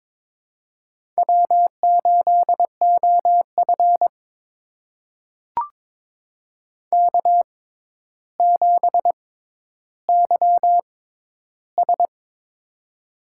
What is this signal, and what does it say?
Morse code, heard as a keyed tone, about 21 wpm, spaced as for 11 wpm, 700 Hz. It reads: W8OF K7YS
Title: USA Call Signs - 4 Characters - 6x Character Spacing - 22wpm